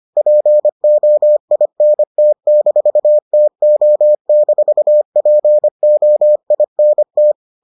Morse code, heard as a keyed tone, about 25 wpm, 600 Hz. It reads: POINT-TO-POINT